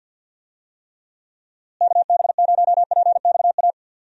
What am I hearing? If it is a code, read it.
Morse code, heard as a keyed tone, about 51 wpm, 700 Hz. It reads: KB0PXA
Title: USA Call Signs - 6 Characters - Repeated 3x - 50wpm